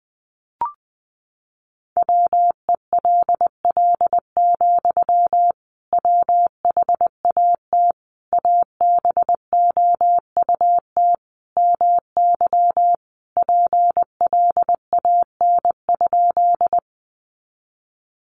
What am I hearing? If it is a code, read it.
Morse code, heard as a keyed tone, about 19 wpm, 700 Hz. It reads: WELL, WHAT ABOUT MY PLAN?
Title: Sentences from Top 500 Words 20wpm